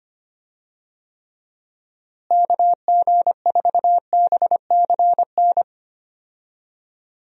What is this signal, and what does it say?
Morse code, heard as a keyed tone, about 25 wpm, 700 Hz. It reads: KG4BCN
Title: USA Call Signs - 6 Characters - Repeated 3x - 25wpm